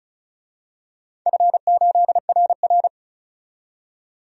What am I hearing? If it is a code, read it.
Morse code, heard as a keyed tone, about 36 wpm, 700 Hz. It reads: F8RR
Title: DXCC-100 Call Signs - 4 Characters - 35wpm